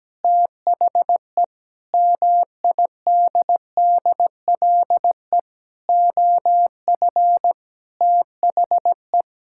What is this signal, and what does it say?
Morse code, heard as a keyed tone, about 17 wpm, 700 Hz. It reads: THE MIDDLE OF THE
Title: Sentences from Top 1000 Words - Review Part 6 - 17wpm